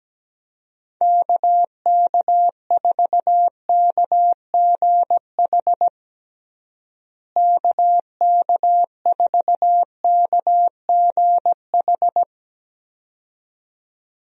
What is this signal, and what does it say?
Morse code, heard as a keyed tone, about 17 wpm, 700 Hz. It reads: KK4KGH KK4KGH
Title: USA Call Signs - 6 Characters - Repeated 2x - 17wpm